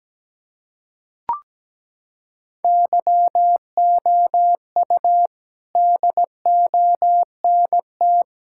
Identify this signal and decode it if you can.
Morse code, heard as a keyed tone, about 17 wpm, 700 Hz. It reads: YOU DONT
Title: Sentences from Top 1000 Words - Review Part 6 - 17wpm